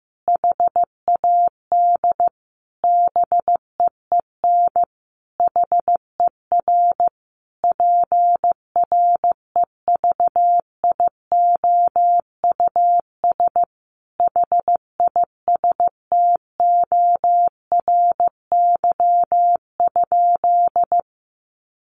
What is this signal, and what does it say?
Morse code, heard as a keyed tone, about 15 wpm, 700 Hz. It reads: HAD BEEN HER PREVIOUS HISTORY?
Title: Sentences from Top 1500 Words 15wpm